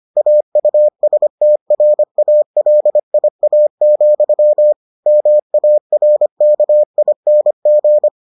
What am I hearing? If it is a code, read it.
Morse code, heard as a keyed tone, about 25 wpm, 600 Hz. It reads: AUSTRALIA, MARKING